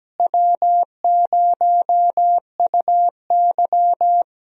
Morse code, heard as a keyed tone, about 17 wpm, 700 Hz. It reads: W0UY
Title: USA Call Signs - 4 Characters - Repeated 3x - 17wpm